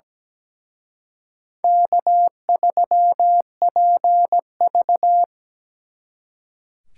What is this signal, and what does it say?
Morse code, heard as a keyed tone, about 20 wpm, 700 Hz. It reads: K3PV
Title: Speed Racing - USA 1x2 Call Signs 17wpm